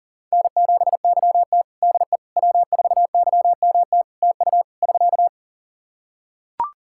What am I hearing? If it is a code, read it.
Morse code, heard as a keyed tone, about 37 wpm, 700 Hz. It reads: N7YT DE W4YMT TU <SK>
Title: K1USN SST QSO Practice 40wpm